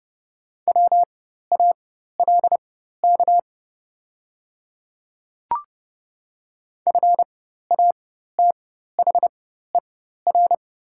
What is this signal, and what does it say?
Morse code, heard as a keyed tone, about 30 wpm, spaced as for 14 wpm, 700 Hz. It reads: WALK FATHER
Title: 4x Character Spacing - Top 300 Words - 30wpm